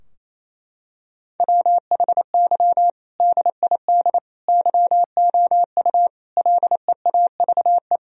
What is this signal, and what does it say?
Morse code, heard as a keyed tone, about 27 wpm, 700 Hz. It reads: WHY DID YOU LEAVE
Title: Sentences from Top 1000 Words - Review Part 3 - 28wpm